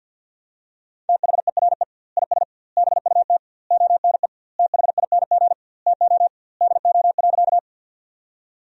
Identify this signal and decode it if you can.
Morse code, heard as a keyed tone, about 48 wpm, 700 Hz. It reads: THERE IS BUT ONE THING TO DO.
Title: 2x Word-Spacing - Sentences from Top 100 Words - 50wpm